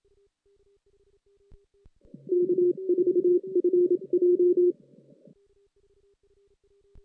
Morse code, heard as a keyed tone, about 27 wpm, 380 Hz. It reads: K4FJ
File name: 2016 JAN CW NAQP